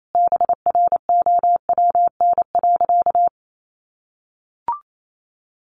Morse code, heard as a keyed tone, about 28 wpm, 700 Hz. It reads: BROWN.
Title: Sentences from Top 1200 Words 28wpm